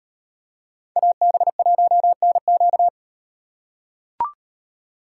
Morse code, heard as a keyed tone, about 38 wpm, 700 Hz. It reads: AB1NQ